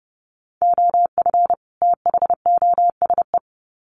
Morse code, heard as a keyed tone, about 30 wpm, 700 Hz. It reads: OF THOSE
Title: Sets of 2 Words 30wpm